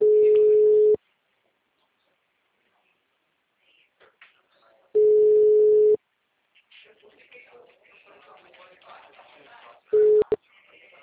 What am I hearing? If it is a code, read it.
Ringback tone (Europe, 425 Hz), with speech.
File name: out- -2002- - - .95592